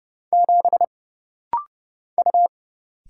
Morse code, heard as a keyed tone, about 29 wpm, 700 Hz. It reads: 7 U
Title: Single Letter-Number - Rapid-Fire 30wpm